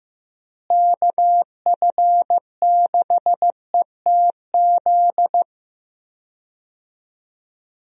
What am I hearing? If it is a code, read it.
Morse code, heard as a keyed tone, about 15 wpm, 700 Hz. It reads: KF6ETZ
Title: USA Call Signs - 6 Characters - Repeated 3x - 15wpm